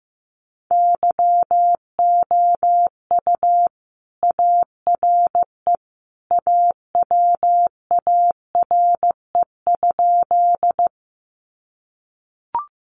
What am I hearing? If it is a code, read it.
Morse code, heard as a keyed tone, about 15 wpm, 700 Hz. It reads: YOU ARE AWARE?
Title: Sentences from Top 1200 Words 15wpm